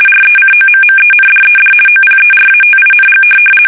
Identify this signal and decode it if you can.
RTTY, about 50 baud. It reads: RECOMMENDED OPERATING
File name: fsk850-50B / teletype-corp